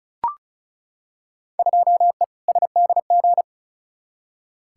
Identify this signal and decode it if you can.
Morse code, heard as a keyed tone, about 36 wpm, 700 Hz. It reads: 2E SDG
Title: Field Day Exchanges - 35wpm